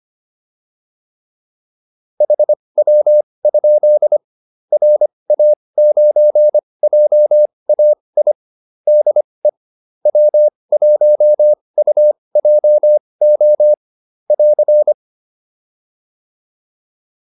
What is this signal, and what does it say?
Morse code, heard as a keyed tone, about 25 wpm, spaced as for 20 wpm, 600 Hz. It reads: HW? RA9JAI DE W1UJO <AR>